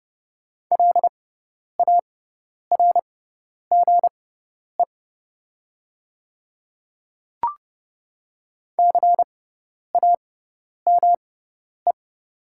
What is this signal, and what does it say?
Morse code, heard as a keyed tone, about 31 wpm, spaced as for 10 wpm, 700 Hz. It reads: LARGE CAME
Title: 6x Character Spacing - Top 300 Words - 30wpm